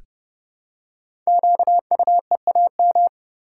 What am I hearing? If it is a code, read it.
Morse code, heard as a keyed tone, about 30 wpm, 700 Hz. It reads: QUEAM
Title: Code Groups - 5 Letters - 30wpm